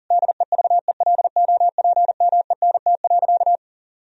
Morse code, heard as a keyed tone, about 40 wpm, 700 Hz. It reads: DEVELOPMENT.
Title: Sentences from Top 1000 Words - Review Part 4 - 40wpm